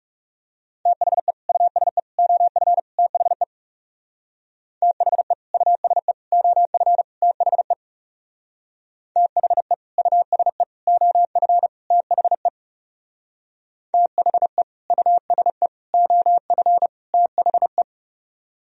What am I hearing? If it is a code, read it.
Morse code, heard as a keyed tone, about 40 wpm, 700 Hz. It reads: THE USE OF THE THE USE OF THE THE USE OF THE THE USE OF THE